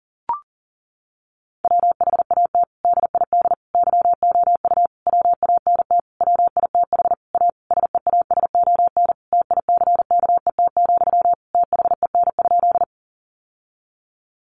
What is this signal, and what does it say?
Morse code, heard as a keyed tone, about 40 wpm, 700 Hz. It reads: WHAT DID YOU WANT WITH A SEASON TICKET, THEN?
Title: Sentences from Top 1400 Words 40wpm